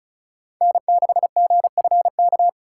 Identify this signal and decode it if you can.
Morse code, heard as a keyed tone, about 46 wpm, 700 Hz. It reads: N6GFK